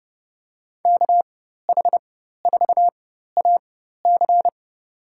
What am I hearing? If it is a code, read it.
Morse code, heard as a keyed tone, about 30 wpm, spaced as for 14 wpm, 700 Hz. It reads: KH4AC
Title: USA Call Signs - 4x Character Spacing - 30wpm